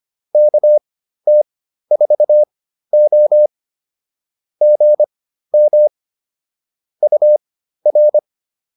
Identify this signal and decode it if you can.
Morse code, heard as a keyed tone, about 25 wpm, spaced as for 13 wpm, 600 Hz. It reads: KT4O GM UR